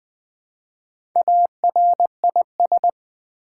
Morse code, heard as a keyed tone, about 20 wpm, 700 Hz. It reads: ARIS
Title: Build A Word - 1 Letter At A Time - 5-Letter Words - 20wpm